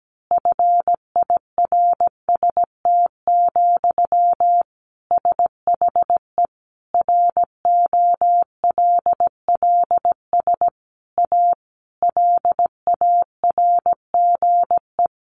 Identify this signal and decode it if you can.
Morse code, heard as a keyed tone, about 17 wpm, 700 Hz. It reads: FIRST, SHE ROLLS A LARGE
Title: Easy Phrases 17wpm